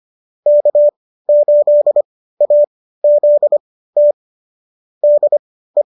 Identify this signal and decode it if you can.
Morse code, heard as a keyed tone, about 25 wpm, spaced as for 15 wpm, 570 Hz. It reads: K8AZT DE